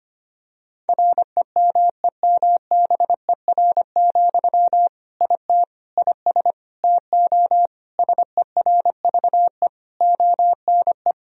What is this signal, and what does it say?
Morse code, heard as a keyed tone, about 25 wpm, 700 Hz. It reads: REMEMBER, IT IS TO SERVE ONE
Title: Sentences from Top 1000 Words - Review Part 2 - 25wpm